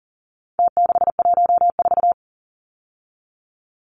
Morse code, heard as a keyed tone, about 40 wpm, 700 Hz. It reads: T614
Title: Numbers - Contest Style - 4-Digit 40wpm